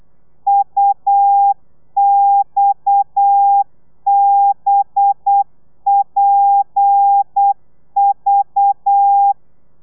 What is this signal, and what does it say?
Morse code, heard as a keyed tone, about 8 wpm, 800 Hz. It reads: UXBPV